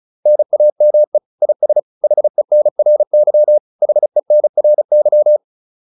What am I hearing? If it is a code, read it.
Morse code, heard as a keyed tone, about 35 wpm, 600 Hz. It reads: NAME IS HENRY HENRY
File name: QSO102 35